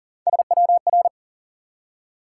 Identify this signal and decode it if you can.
Morse code, heard as a keyed tone, about 42 wpm, 700 Hz. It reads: SWR
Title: Top 180 QSO Elements 40wpm